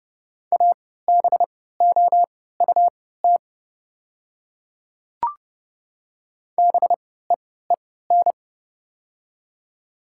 Morse code, heard as a keyed tone, about 30 wpm, spaced as for 17 wpm, 700 Hz. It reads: ABOUT BEEN
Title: 3x Character Spacing - Top 100 Words - 30wpm